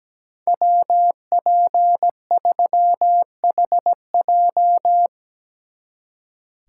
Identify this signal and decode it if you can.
Morse code, heard as a keyed tone, about 18 wpm, 700 Hz. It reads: WP3HJ
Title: Speed Racing - USA 2x2 Call Signs 17wpm